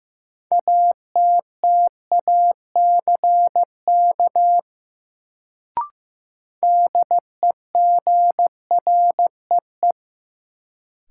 Morse code, heard as a keyed tone, about 15 wpm, 700 Hz. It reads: ATTACK DEGREE
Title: Sentences from Top 1500 Words - New Word Drill Alt - 15wpm